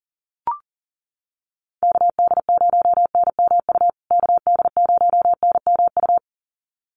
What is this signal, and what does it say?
Morse code, heard as a keyed tone, about 40 wpm, 700 Hz. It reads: KD0NMU KD0NMU
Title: USA Call Signs Repeated 2x 40wpm